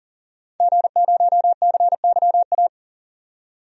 Morse code, heard as a keyed tone, about 40 wpm, 700 Hz. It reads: G0CYA
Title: DXCC-100 Call Signs - 5 Characters - 40wpm